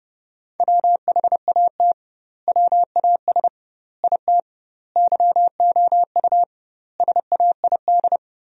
Morse code, heard as a keyed tone, about 32 wpm, 700 Hz. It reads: WHAT WAS IT YOU SAID
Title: 2x Word-Spacing - Sentences from Top 200 Words - 30wpm